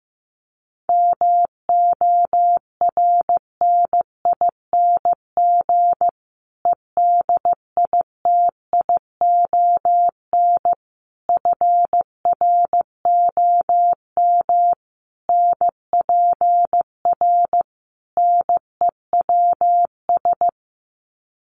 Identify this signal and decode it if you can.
Morse code, heard as a keyed tone, about 15 wpm, 700 Hz. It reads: MORNING EDITION FROM NPR NEWS
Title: Sets of 5 Words Encore 15wpm